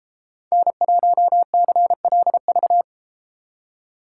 Morse code, heard as a keyed tone, about 32 wpm, 700 Hz. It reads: N1CLV